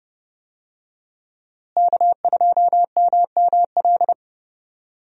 Morse code, heard as a keyed tone, about 32 wpm, 700 Hz. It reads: K2MML
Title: USA Call Signs - 5 Characters - Repeated 3x - 30wpm